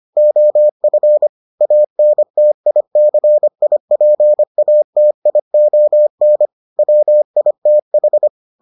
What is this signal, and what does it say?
Morse code, heard as a keyed tone, about 25 wpm, 600 Hz. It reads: OF ANTICIPATION WITH